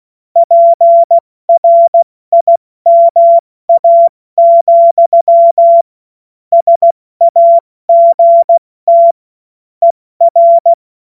Morse code, heard as a keyed tone, about 16 wpm, 680 Hz. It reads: PRIMA, SAGT ER
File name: Morsekurs / Witze / Witz 12 März 2024 70 F80